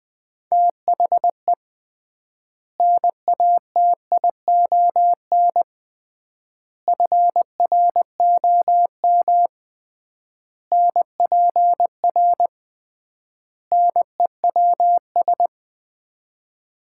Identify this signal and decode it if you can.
Morse code, heard as a keyed tone, about 20 wpm, 700 Hz. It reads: THE NATION FROM NPR NEWS
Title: 3x Word-Spacing - Sets of 5 Words - 20wpm